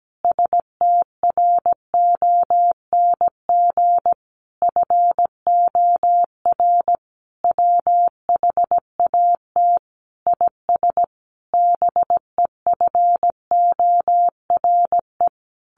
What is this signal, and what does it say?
Morse code, heard as a keyed tone, about 17 wpm, 700 Hz. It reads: STRONG FOR WHAT IS BEFORE
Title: Sentences from Top 600 Words 17wpm